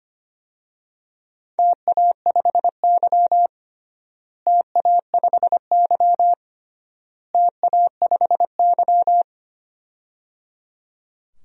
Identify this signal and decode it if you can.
Morse code, heard as a keyed tone, about 25 wpm, 700 Hz. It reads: TA5Y TA5Y TA5Y
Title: DXCC-100 Call Signs - 4 Characters - Repeated 3x - 25wpm